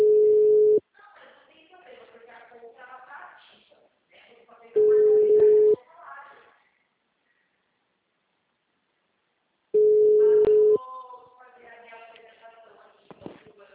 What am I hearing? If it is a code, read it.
Ringback tone (Europe, 425 Hz), with speech.
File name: out- -2013- - - .29927